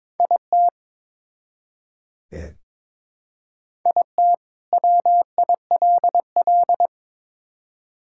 Morse code, heard as a keyed tone, about 22 wpm, 700 Hz. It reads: IT IT WILL
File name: Build A Phrase - 2-Word Phrases - 22wpm